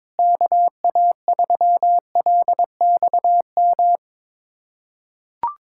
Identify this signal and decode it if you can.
Morse code, heard as a keyed tone, about 22 wpm, 700 Hz. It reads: KA3LXM
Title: USA Callsign DE Callsign - 6 Characters - 22wpm